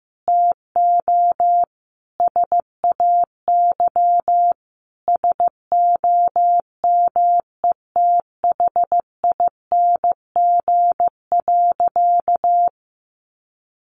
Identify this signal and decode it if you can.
Morse code, heard as a keyed tone, about 15 wpm, 700 Hz. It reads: TO SAY SOMETHING.
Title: Sentences from Top 600 Words 15wpm